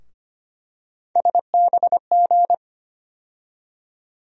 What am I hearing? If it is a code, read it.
Morse code, heard as a keyed tone, about 25 wpm, 700 Hz. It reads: SBG
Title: USA Call Sign Postfixes - 3 Letters 25wpm